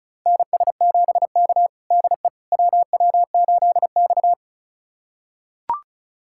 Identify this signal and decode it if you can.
Morse code, heard as a keyed tone, about 34 wpm, 700 Hz. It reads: NS7K DE WW8X
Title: USA Callsign DE Callsign - 4 Characters - 35wpm